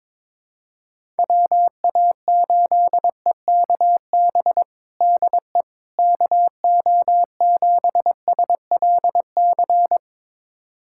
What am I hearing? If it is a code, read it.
Morse code, heard as a keyed tone, about 22 wpm, 700 Hz. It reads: WA8EKB DE KO7SLC
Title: USA Callsign DE Callsign - 6 Characters - 22wpm